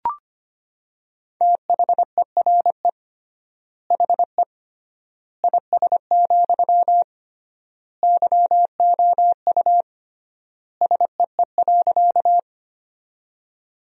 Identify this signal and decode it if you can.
Morse code, heard as a keyed tone, about 25 wpm, 700 Hz. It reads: THERE HE IS, YOU SEE.
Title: 3x Word-Spacing - Sentences from Top 100 Words - 25wpm